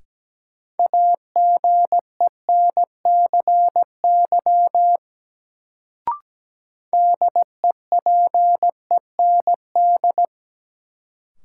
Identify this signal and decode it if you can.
Morse code, heard as a keyed tone, about 17 wpm, 700 Hz. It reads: AGENCY DEPEND
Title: Sentences from Top 1300 Words - New Word Drill - 17wpm